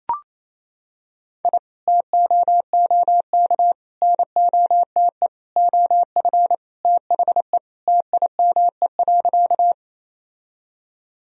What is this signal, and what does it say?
Morse code, heard as a keyed tone, about 27 wpm, 700 Hz. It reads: I TOOK NOTE OF THE TIME.
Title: Sentences from Top 1000 Words - Review Part 2 - 28wpm